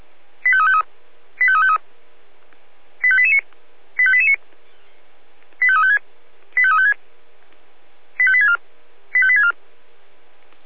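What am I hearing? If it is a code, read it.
Five-tone selective call (ZVEI), heard as quick runs of five tones, tones of about 70 ms. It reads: 85353 85353 85889 85889 85356 85356 85864 85864
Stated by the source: ZVEI five-tone call received on an SDR.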